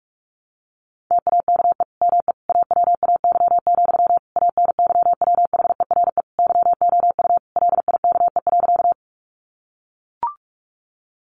Morse code, heard as a keyed tone, about 44 wpm, 700 Hz. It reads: TAKE ME AWAY, ANYWHERE YOU LIKE.
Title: Sentences from Top 1400 Words 45wpm